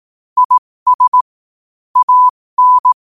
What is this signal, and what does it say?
Morse code, heard as a keyed tone, about 18 wpm, spaced as for 15 wpm, 1000 Hz. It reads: IS AN
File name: qotdfiles / 15wpm